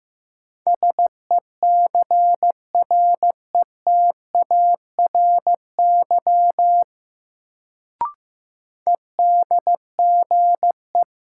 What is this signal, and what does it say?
Morse code, heard as a keyed tone, about 15 wpm, 700 Hz. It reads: SECRETARY EDGE
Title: Sentences from Top 1300 Words - New Word Drill Alt - 15wpm